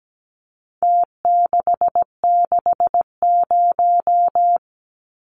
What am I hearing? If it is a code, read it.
Morse code, heard as a keyed tone, about 17 wpm, 700 Hz. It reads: T660
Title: Numbers - Contest Style - 4-Digit 17wpm